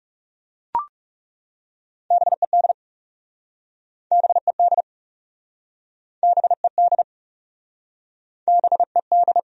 Morse code, heard as a keyed tone, about 41 wpm, 700 Hz. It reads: BED BED BED BED